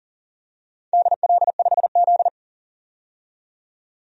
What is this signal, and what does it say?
Morse code, heard as a keyed tone, about 40 wpm, 700 Hz. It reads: DL5Z
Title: DXCC-100 Call Signs - 4 Characters - 40wpm